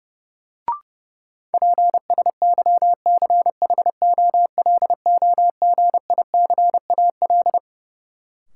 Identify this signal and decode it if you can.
Morse code, heard as a keyed tone, about 30 wpm, 700 Hz. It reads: PSYCHOLOGICAL
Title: Sentences from Top 1400 Words - New Word Drill - 30wpm